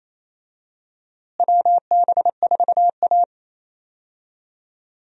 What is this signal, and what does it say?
Morse code, heard as a keyed tone, about 28 wpm, 700 Hz. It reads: WB4A
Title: USA Call Signs - 4 Characters - Repeated 3x - 28wpm